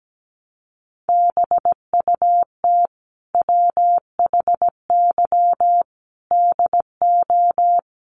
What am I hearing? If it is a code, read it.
Morse code, heard as a keyed tone, about 17 wpm, 700 Hz. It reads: BUT WHY DO
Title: Sentences from Top 1200 Words 17wpm